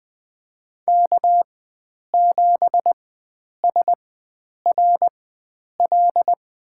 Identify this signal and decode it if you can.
Morse code, heard as a keyed tone, about 20 wpm, spaced as for 9 wpm, 700 Hz. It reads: K7SRL
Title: USA Call Signs - 4x Character Spacing - 20wpm